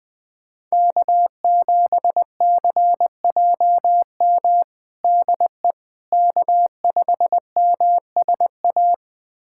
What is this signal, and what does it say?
Morse code, heard as a keyed tone, about 21 wpm, 700 Hz. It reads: K7CJM DE K5MSA
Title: USA Callsign DE Callsign - 5 Characters - 20wpm